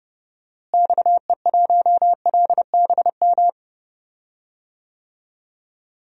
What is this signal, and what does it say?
Morse code, heard as a keyed tone, about 30 wpm, 700 Hz. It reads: XE1LBM
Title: DXCC-100 Call Signs - 6 Characters - Repeated 2x - 30wpm